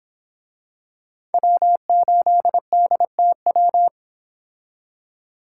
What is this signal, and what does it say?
Morse code, heard as a keyed tone, about 26 wpm, 700 Hz. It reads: W8DTW
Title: Speed Racing - USA 1x3 Call Signs 22wpm